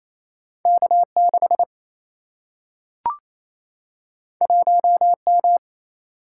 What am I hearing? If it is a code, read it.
Morse code, heard as a keyed tone, about 28 wpm, 700 Hz. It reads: K 6 1 M
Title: Single-Letter Single-Number 28wpm